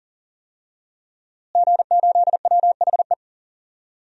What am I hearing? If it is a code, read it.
Morse code, heard as a keyed tone, about 38 wpm, 700 Hz. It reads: G8WHE
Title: DXCC-100 Call Signs - 5 Characters - Repeated 2x - 40wpm